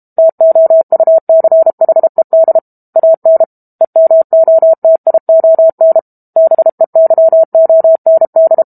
Morse code, heard as a keyed tone, about 32 wpm, 650 Hz. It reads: TOUCHED AN EMOTION BEYOND
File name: Book Two / 06-Chapter 6 - 32 WPM